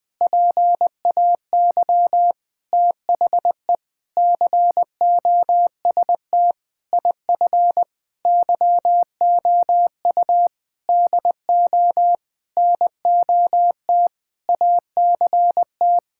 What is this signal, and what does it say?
Morse code, heard as a keyed tone, about 21 wpm, 700 Hz. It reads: PAY THE COST IF YOU DO NOT ACT
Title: Sentences with 8-Word Phrases - 20wpm